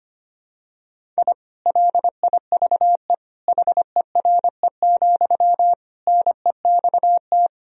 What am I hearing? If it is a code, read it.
Morse code, heard as a keyed tone, about 25 wpm, 700 Hz. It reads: I LIVE HERE, NEXT
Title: Sentences from Top 1000 Words - Review Part 1 - 25wpm